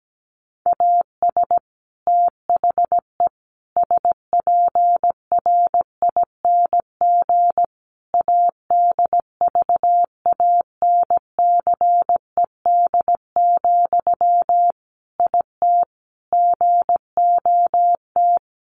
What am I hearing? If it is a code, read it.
Morse code, heard as a keyed tone, about 17 wpm, 700 Hz. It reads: AS THE SPRING ADVANCED, IT GOT
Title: Sentences from Top 1200 Words 17wpm